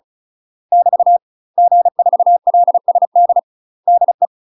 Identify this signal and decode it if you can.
Morse code, heard as a keyed tone, about 36 wpm, 700 Hz. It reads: <BT> G4LSD DE
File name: UK MASTER QSOs TRAINING / 35WPM / UK MASTER QSOs 0018